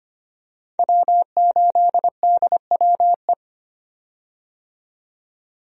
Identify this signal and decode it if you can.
Morse code, heard as a keyed tone, about 25 wpm, 700 Hz. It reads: W8DWE
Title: USA Call Signs - 5 Characters - 25wpm